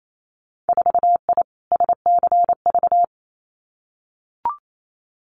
Morse code, heard as a keyed tone, about 28 wpm, 690 Hz. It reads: 4I SCV
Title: Winter Field Day Exchanges Encore 28wpm